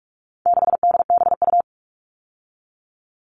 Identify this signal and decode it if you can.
Morse code, heard as a keyed tone, about 44 wpm, 700 Hz. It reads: 6DBU
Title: All Cut Numbers - 4-Digit 45wpm